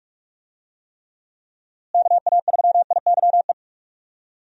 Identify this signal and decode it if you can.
Morse code, heard as a keyed tone, about 46 wpm, 700 Hz. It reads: KA3IYE
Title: USA Call Signs - 6 Characters - Repeated 2x - 45wpm